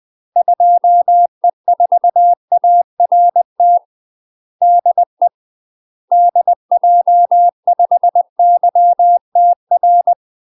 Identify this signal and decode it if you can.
Morse code, heard as a keyed tone, about 20 wpm, 700 Hz. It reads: 2E4ART DE DJ5YTR
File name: UK MASTER QSOs TRAINING / 20WPM / UK MASTER QSOs 0002